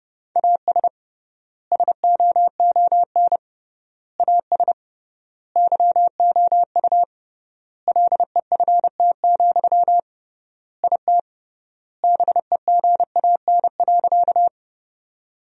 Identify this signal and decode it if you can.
Morse code, heard as a keyed tone, about 29 wpm, 700 Hz. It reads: AS SOON AS YOU LEFT, IT BEGAN.
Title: 3x Word-Spacing - Sentences from Top 400 Words - 30wpm